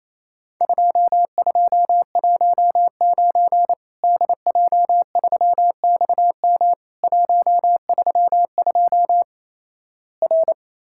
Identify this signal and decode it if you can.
Morse code, heard as a keyed tone, about 28 wpm, 700 Hz. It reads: 2219 DJ3XM 132 R
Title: Worked All Europe DX Contest - QTC Exchange - 28wpm